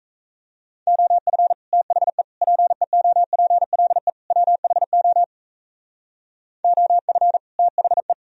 Morse code, heard as a keyed tone, about 35 wpm, 700 Hz. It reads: OF THE PEOPLE WHO OF THE
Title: Speed-Racing - Sets of 4 Words 28wpm